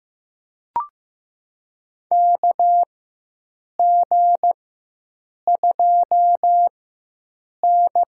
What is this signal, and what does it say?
Morse code, heard as a keyed tone, about 15 wpm, spaced as for 7 wpm, 700 Hz. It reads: KG2N
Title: USA Call Signs - 4x Character Spacing - 15wpm